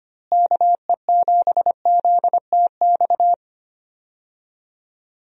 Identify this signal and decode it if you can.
Morse code, heard as a keyed tone, about 25 wpm, 700 Hz. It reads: KE7ZTX
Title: USA Call Signs - 6 Characters - Repeated 3x - 25wpm